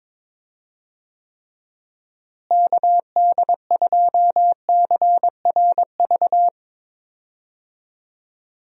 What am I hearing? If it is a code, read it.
Morse code, heard as a keyed tone, about 22 wpm, 700 Hz. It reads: KD2CRV
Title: USA Call Signs - 6 Characters - Repeated 2x - 22wpm